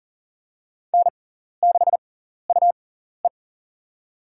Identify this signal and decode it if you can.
Morse code, heard as a keyed tone, about 39 wpm, spaced as for 14 wpm, 700 Hz. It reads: N6UE
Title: USA Call Signs - 4 Characters - 6x Character Spacing - 40wpm